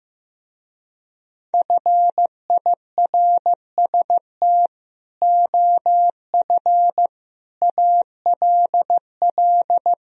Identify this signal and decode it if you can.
Morse code, heard as a keyed tone, about 15 wpm, 700 Hz. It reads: FIRST OF ALL
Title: Speed-Racing - Sets of 3 Words 15wpm